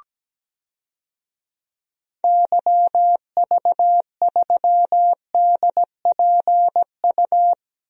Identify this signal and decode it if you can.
Morse code, heard as a keyed tone, about 17 wpm, 700 Hz. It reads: YV3DPU
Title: DXCC-100 Call Signs - 6 Characters - 17wpm